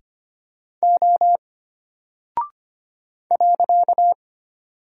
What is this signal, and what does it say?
Morse code, heard as a keyed tone, about 25 wpm, 700 Hz. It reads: O .
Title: Single Character (Letter:Number:Punctuation:Prosign) Alt 25wpm